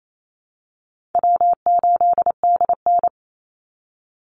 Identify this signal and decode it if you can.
Morse code, heard as a keyed tone, about 28 wpm, 700 Hz. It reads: W8DN
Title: Call Signs Repeated 2x - 28wpm